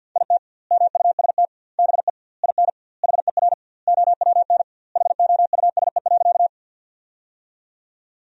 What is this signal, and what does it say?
Morse code, heard as a keyed tone, about 50 wpm, 700 Hz. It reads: IT MUST BE IN HER OWN HOUSE.
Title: 2x Word-Spacing - Sentences from Top 300 Words - 50wpm